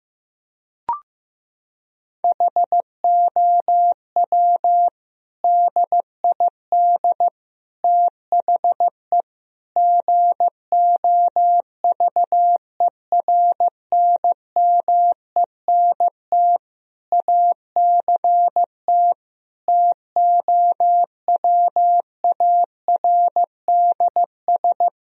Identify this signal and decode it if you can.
Morse code, heard as a keyed tone, about 15 wpm, 700 Hz. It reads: HOW DID THE GOVERNMENT ACT TOWARDS
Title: Sentences from Top 1000 Words - Review Part 4 - 15wpm